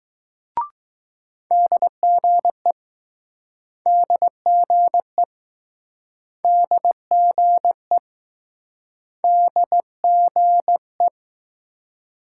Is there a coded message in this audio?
Yes: Morse code from 1.5 to 11.1 s, heard as a keyed tone, about 20 wpm, 700 Hz. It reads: DGE DGE DGE DGE